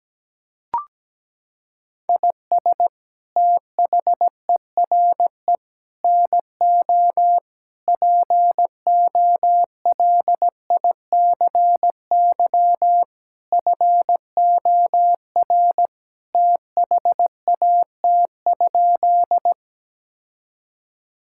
Morse code, heard as a keyed tone, about 17 wpm, 700 Hz. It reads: IS THERE NO POLICY FOR THAT?
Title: Sentences from Top 1000 Words - Review Part 5 - 17wpm